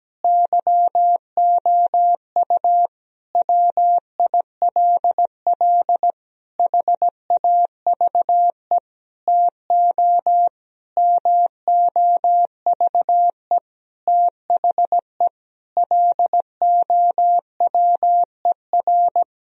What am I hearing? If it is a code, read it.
Morse code, heard as a keyed tone, about 17 wpm, 700 Hz. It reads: YOU WILL HAVE TO MOVE THE LOWER
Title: Sentences from Top 1000 Words - Review Part 4 - 17wpm